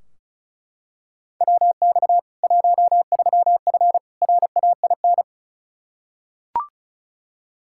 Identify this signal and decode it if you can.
Morse code, heard as a keyed tone, about 34 wpm, 700 Hz. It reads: WX 13F RAIN
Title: Practice QSO - Exchange 2 - Info - 35wpm